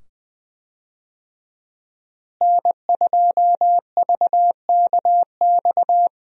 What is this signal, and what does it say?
Morse code, heard as a keyed tone, about 21 wpm, 700 Hz. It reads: N2VKX
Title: USA Call Signs - 5 Characters - Repeated 2x - 20wpm